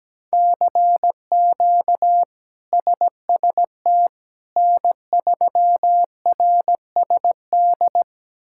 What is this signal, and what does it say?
Morse code, heard as a keyed tone, about 17 wpm, 700 Hz. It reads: CQ SST N3RSD
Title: K1USN SST QSO Practice - 17wpm